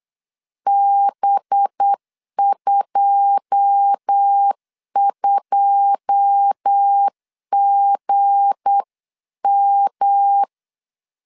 Morse code, heard as a keyed tone, about 8 wpm, 800 Hz. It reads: B22GM
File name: CW / DET / 1 9 1500C S B 8wpm 5624